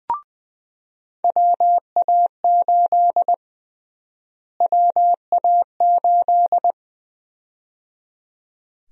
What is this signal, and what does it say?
Morse code, heard as a keyed tone, about 20 wpm, 700 Hz. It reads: WA8 WA8
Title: USA Call Sign Prefixes - Repeated 2x - 20wpm